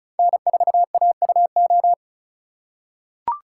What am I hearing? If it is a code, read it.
Morse code, heard as a keyed tone, about 36 wpm, 700 Hz. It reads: N4AUO
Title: USA Call Signs - 5 Characters - 4x Character Spacing - 35wpm